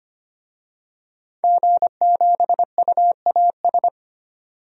Morse code, heard as a keyed tone, about 25 wpm, 700 Hz. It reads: G7UAS
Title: DXCC-100 Call Signs - 5 Characters - 25wpm